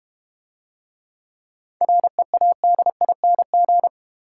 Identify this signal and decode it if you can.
Morse code, heard as a keyed tone, about 32 wpm, 700 Hz. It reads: READING